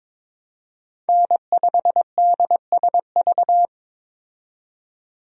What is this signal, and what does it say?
Morse code, heard as a keyed tone, about 22 wpm, 700 Hz. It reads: N5DSV
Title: USA Call Signs - 5 Characters - 22wpm